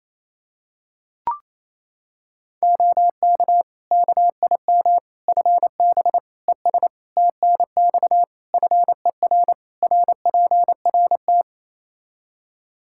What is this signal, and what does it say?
Morse code, heard as a keyed tone, about 28 wpm, 700 Hz. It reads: OK KIM FB ES TNX FER RPRT
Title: Practice QSO - Exchange 2 - Info - 28wpm